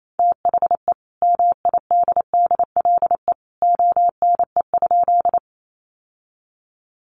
Morse code, heard as a keyed tone, about 28 wpm, 700 Hz. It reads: THE MIDDLE ONE?
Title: Sentences from Top 900 Words 28wpm